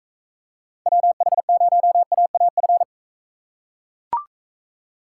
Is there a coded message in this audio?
Yes: Morse code from 0.9 to 2.8 s, heard as a keyed tone, about 40 wpm, 700 Hz. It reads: WH0AAF